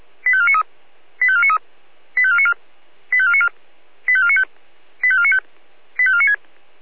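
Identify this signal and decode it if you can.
Five-tone selective call (ZVEI), heard as quick runs of five tones, tones of about 70 ms. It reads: 85493 85493 85494 85494 85495 85495 85496
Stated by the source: ZVEI five-tone selective call 85491 received with an SDR.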